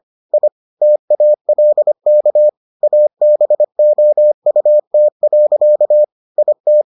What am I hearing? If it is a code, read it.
Morse code, heard as a keyed tone, about 25 wpm, 600 Hz. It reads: I TALK ABOUT. IT